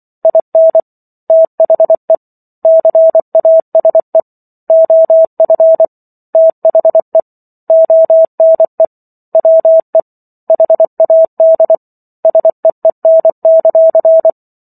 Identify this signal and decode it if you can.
Morse code, heard as a keyed tone, about 24 wpm, 650 Hz. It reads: IN THE CASE OF THE ONE WE HAD SEEN;